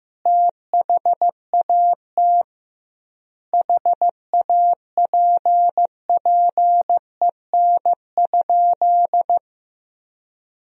Morse code, heard as a keyed tone, about 15 wpm, 700 Hz. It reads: THAT HAPPEN?
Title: 2x Word-Spacing - Sentences from Top 400 Words - 15wpm